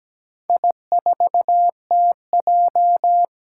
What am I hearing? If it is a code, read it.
Morse code, heard as a keyed tone, about 17 wpm, 700 Hz. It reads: I4TJ
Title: Four Challenging Letter-Number Combinations 17wpm